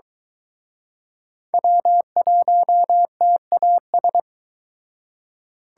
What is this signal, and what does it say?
Morse code, heard as a keyed tone, about 25 wpm, 700 Hz. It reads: W1TAS